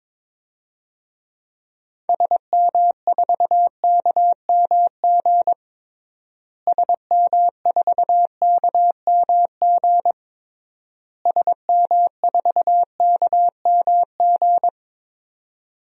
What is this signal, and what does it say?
Morse code, heard as a keyed tone, about 22 wpm, 700 Hz. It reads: SM4KMG SM4KMG SM4KMG
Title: DXCC-100 Call Signs - 6 Characters - Repeated 3x - 22wpm